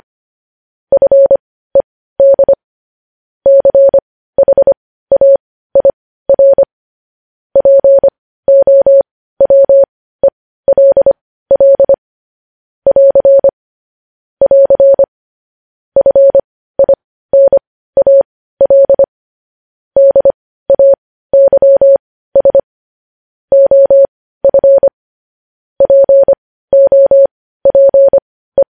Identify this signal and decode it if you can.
Morse code, heard as a keyed tone, about 25 wpm, spaced as for 15 wpm, 550 Hz. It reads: FED CHAIR POWELL <AR> <AR> FINAL DAYS OF POPE
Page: News headlines in morse code at 15 wpm